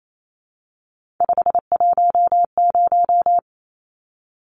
Morse code, heard as a keyed tone, about 28 wpm, 700 Hz. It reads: 510
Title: Numbers - 3-Digit 28wpm